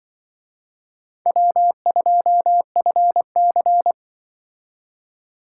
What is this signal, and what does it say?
Morse code, heard as a keyed tone, about 24 wpm, 700 Hz. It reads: W2FC
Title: Speed Racing - USA 1x2 Call Signs 20wpm